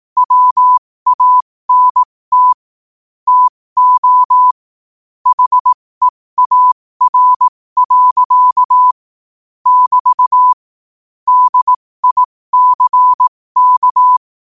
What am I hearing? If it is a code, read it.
Morse code, heard as a keyed tone, about 18 wpm, spaced as for 15 wpm, 1000 Hz. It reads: WANT TO HEAR. <BT> DICK
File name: qotdfiles / 15wpm